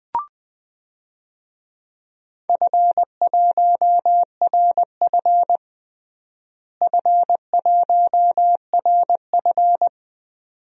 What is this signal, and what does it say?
Morse code, heard as a keyed tone, about 20 wpm, 700 Hz. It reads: F1RF F1RF
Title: DXCC-100 Call Signs - 4 Characters - Repeated 2x - 20wpm